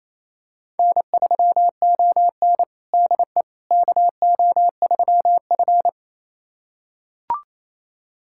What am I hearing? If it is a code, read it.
Morse code, heard as a keyed tone, about 27 wpm, 690 Hz. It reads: N3ON DE KO3F
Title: USA Callsign DE Callsign - 4 Characters - 28wpm